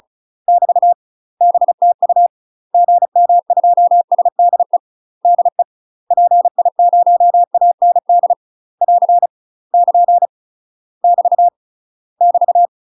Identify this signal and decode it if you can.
Morse code, heard as a keyed tone, about 34 wpm, 700 Hz. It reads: <BT> BTU GM2SDE DE PI0AND <AR> <KN> <BT> <BT>
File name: UK MASTER QSOs TRAINING / 35WPM / UK MASTER QSOs 0017